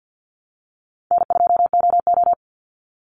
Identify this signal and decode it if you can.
Morse code, heard as a keyed tone, about 50 wpm, 700 Hz. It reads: N2OO
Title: Easy Call Signs Repeated 2x - 50wpm